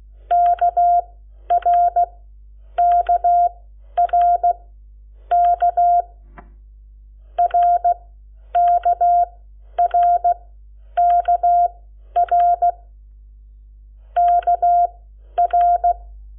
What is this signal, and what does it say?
Morse code, heard as a keyed tone, about 16 wpm, spaced as for 10 wpm, 700 Hz. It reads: KRKRK RKRKR KR
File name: Lecciones / CLASE Nº7